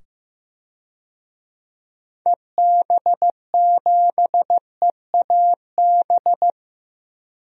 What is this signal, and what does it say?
Morse code, heard as a keyed tone, about 15 wpm, 700 Hz. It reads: EB7EAB
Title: DXCC-100 Call Signs - 6 Characters - 15wpm